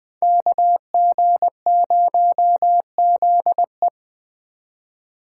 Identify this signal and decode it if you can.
Morse code, heard as a keyed tone, about 20 wpm, 700 Hz. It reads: KG0ZE
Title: Speed Racing - USA 2x2 Call Signs 17wpm